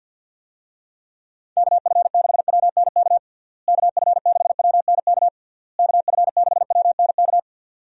Morse code, heard as a keyed tone, about 50 wpm, 700 Hz. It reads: KF6WNK KF6WNK KF6WNK
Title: USA Call Signs - 6 Characters - Repeated 3x - 50wpm